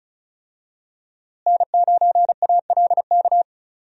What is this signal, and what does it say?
Morse code, heard as a keyed tone, about 36 wpm, 700 Hz. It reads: N9ALK
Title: USA Call Signs - 5 Characters - Repeated 3x - 35wpm